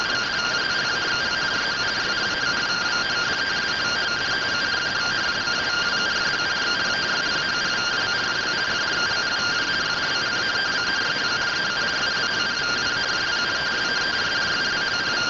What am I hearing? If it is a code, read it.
Two-tone teleprinter signal, about 170 Hz shift.